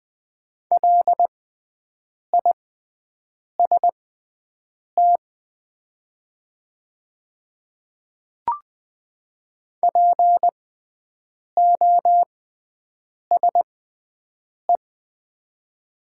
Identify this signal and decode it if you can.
Morse code, heard as a keyed tone, about 20 wpm, spaced as for 7 wpm, 700 Hz. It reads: LIST POSE
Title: 6x Character Spacing - Top 500 Words - 20wpm